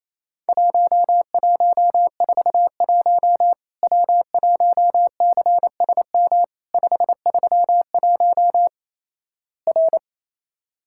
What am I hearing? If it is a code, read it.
Morse code, heard as a keyed tone, about 28 wpm, 700 Hz. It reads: 1141 W1CSM 531 R
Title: Worked All Europe DX Contest - QTC Exchange - 28wpm